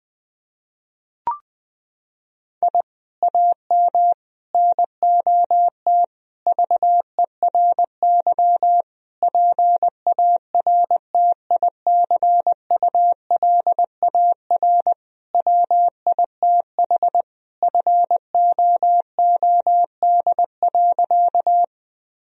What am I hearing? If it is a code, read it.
Morse code, heard as a keyed tone, about 21 wpm, 700 Hz. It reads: I AM NOT VERY PARTICULAR WITH FOOD.
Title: Sentences from Top 1000 Words - Review Part 4 - 20wpm